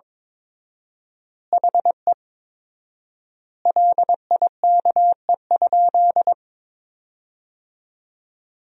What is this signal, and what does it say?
Morse code, heard as a keyed tone, about 22 wpm, 700 Hz. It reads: HE LIKE?
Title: 4x Word-Spacing - Sentences from Top 100 Words - 22wpm